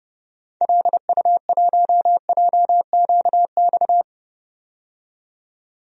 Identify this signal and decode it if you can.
Morse code, heard as a keyed tone, about 30 wpm, 700 Hz. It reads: LU1JQX
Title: DXCC-100 Call Signs - 6 Characters - 30wpm